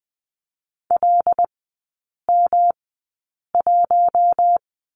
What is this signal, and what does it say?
Morse code, heard as a keyed tone, about 20 wpm, spaced as for 8 wpm, 700 Hz. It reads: LM1
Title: 107 Lesson 18e - Letter m - Three Characters 20wpm